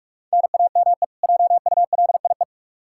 Morse code, heard as a keyed tone, about 46 wpm, 700 Hz. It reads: NAME JULIE
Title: K1USN SST QSO Practice 45wpm